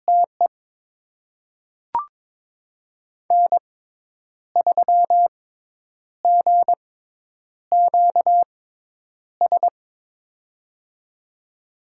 Morse code, heard as a keyed tone, about 22 wpm, spaced as for 9 wpm, 700 Hz. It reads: TE N3GQS
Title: USA Call Signs - 5 Characters - 6x Character Spacing - 22wpm